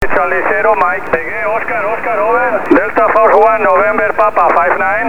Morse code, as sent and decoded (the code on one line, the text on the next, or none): none